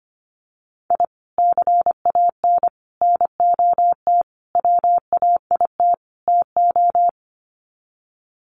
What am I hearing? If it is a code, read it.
Morse code, heard as a keyed tone, about 25 wpm, 700 Hz. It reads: I CAN NOT WAIT TO
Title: Sets of 5 Words Encore 25wpm